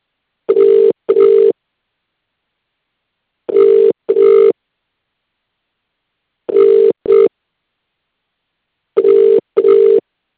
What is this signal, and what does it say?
Ringback tone (UK), 400 + 450 Hz.